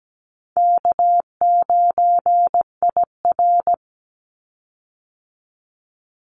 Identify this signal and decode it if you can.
Morse code, heard as a keyed tone, about 17 wpm, 700 Hz. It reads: K9IR
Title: Call Signs - 17wpm